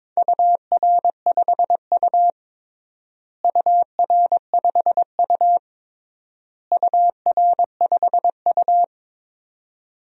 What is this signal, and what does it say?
Morse code, heard as a keyed tone, about 22 wpm, 700 Hz. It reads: UR5U UR5U UR5U
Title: DXCC-100 Call Signs - 4 Characters - Repeated 3x - 22wpm